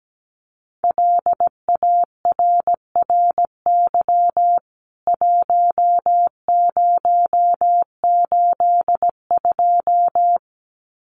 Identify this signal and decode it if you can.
Morse code, heard as a keyed tone, about 17 wpm, 700 Hz. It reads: LARRY 1082
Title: CWT Exchange 17wpm